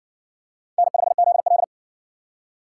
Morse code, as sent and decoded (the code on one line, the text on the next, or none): -. ..... --.. .-..
N5ZL